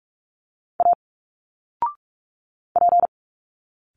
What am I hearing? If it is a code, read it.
Morse code, heard as a keyed tone, about 46 wpm, 700 Hz. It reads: A P
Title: Single Letters 45wpm